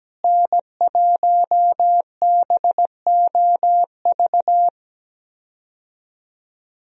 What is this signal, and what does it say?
Morse code, heard as a keyed tone, about 17 wpm, 700 Hz. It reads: N1BOV
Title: USA Call Signs - 5 Characters - Repeated 2x - 17wpm